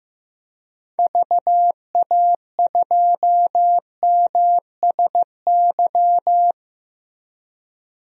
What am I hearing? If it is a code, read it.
Morse code, heard as a keyed tone, about 16 wpm, 700 Hz. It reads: VA2MSY
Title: DXCC-100 Call Signs - 6 Characters - Repeated 3x - 15wpm